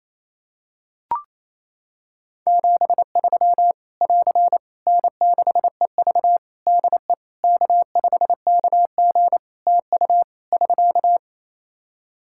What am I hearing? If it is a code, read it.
Morse code, heard as a keyed tone, about 28 wpm, 700 Hz. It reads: 73 <AR> N6EV DE K5KG TU <SK>
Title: K1USN SST QSO Practice 28wpm